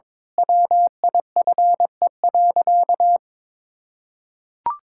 Morse code, heard as a keyed tone, about 22 wpm, 700 Hz. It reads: WIFE.
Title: Pangram Challenge v2 - 22wpm